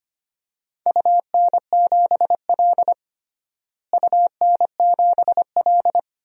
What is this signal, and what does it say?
Morse code, heard as a keyed tone, about 25 wpm, 700 Hz. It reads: UN7L UN7L
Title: DXCC-100 Call Signs - 4 Characters - Repeated 2x - 25wpm